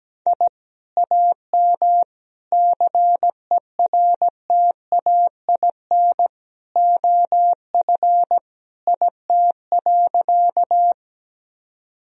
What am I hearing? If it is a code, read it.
Morse code, heard as a keyed tone, about 17 wpm, 700 Hz. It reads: I AM CERTAIN OF IT.
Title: Sentences from Top 1000 Words - Review Part 2 - 17wpm